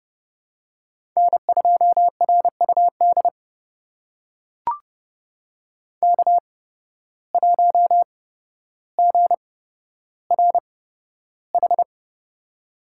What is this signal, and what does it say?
Morse code, heard as a keyed tone, about 29 wpm, spaced as for 13 wpm, 700 Hz. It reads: N2RUD K1GRH
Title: USA Call Signs - 8x Character Spacing - 30wpm